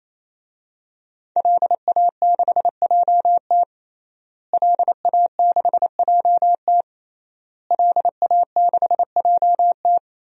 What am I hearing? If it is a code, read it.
Morse code, heard as a keyed tone, about 28 wpm, 700 Hz. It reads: LA6JT LA6JT LA6JT
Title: DXCC-100 Call Signs - 5 Characters - Repeated 3x - 28wpm